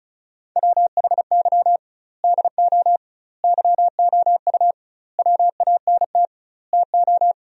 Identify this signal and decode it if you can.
Morse code, heard as a keyed tone, about 36 wpm, 700 Hz. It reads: WHY DO YOU WANT TO
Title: 2x Word-Spacing - Sentences from Top 200 Words - 35wpm